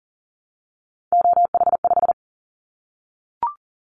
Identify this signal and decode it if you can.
Morse code, heard as a keyed tone, about 40 wpm, 700 Hz. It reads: OH5
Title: Callsign Prefixes for Top 25 DXCC - Repeated 3x - 40wpm